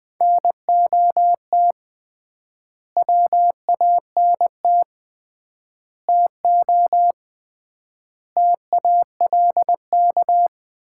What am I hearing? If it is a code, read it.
Morse code, heard as a keyed tone, about 20 wpm, 700 Hz. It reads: NOT WANT TO TALK
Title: 3x Word-Spacing - Sets of 5 Words - 20wpm